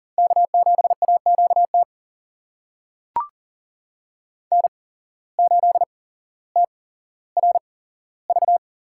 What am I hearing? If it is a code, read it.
Morse code, heard as a keyed tone, about 40 wpm, spaced as for 17 wpm, 700 Hz. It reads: K7AQT N8TRV
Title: USA Call Signs - 5 Characters - 8x Character Spacing - 40wpm